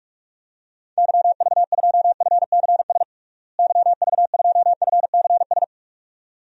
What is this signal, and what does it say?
Morse code, heard as a keyed tone, about 46 wpm, 700 Hz. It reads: YV2FCS YV2FCS
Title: DXCC-100 Call Signs - 6 Characters - Repeated 2x - 45wpm